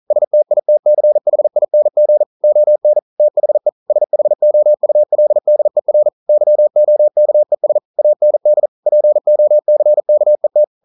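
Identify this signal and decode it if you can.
Morse code, heard as a keyed tone, about 41 wpm, 600 Hz. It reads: STITCHING ON THE SHOULDER YOKES AND POCKET
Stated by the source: Morse code practice at 41 wpm, from CW med Gnister.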